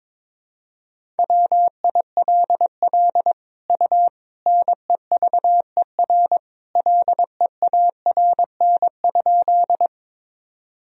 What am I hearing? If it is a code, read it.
Morse code, heard as a keyed tone, about 22 wpm, 700 Hz. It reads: WILL U NEVER LEARN?
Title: Sentences from Top 300 Words and Abbreviations 22wpm